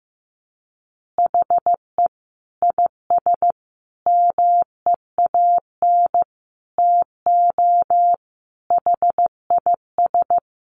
Morse code, heard as a keyed tone, about 15 wpm, 700 Hz. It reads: HE IS MEAN TO HIS
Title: Additional Easy Phrases 15wpm